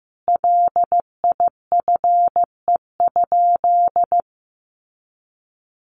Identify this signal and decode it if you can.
Morse code, heard as a keyed tone, about 15 wpm, 700 Hz. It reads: LIFE?
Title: Sentences from Top 1200 Words 15wpm